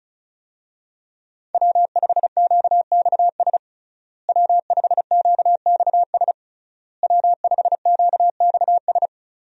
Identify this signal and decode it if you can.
Morse code, heard as a keyed tone, about 34 wpm, 700 Hz. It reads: W5QXS W5QXS W5QXS
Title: USA Call Signs - 5 Characters - Repeated 3x - 35wpm